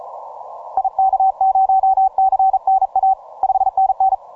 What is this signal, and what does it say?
Morse code, heard as a keyed tone, about 36 wpm, 770 Hz. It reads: IK0CNA 5NN